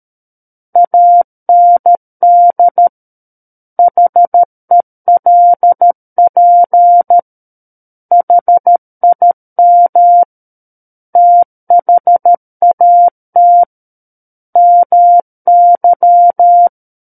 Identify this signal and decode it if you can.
Morse code, heard as a keyed tone, about 13 wpm, 700 Hz. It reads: AND HELP HIM THAT MY
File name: Book One / 04-Chapter 4 - 13 WPM